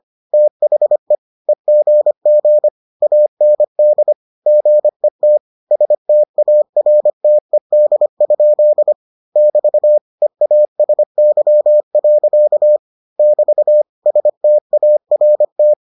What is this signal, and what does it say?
Morse code, heard as a keyed tone, about 25 wpm, 600 Hz. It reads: THE EGG AND GET STARTED? <BT> EASY. <BT> START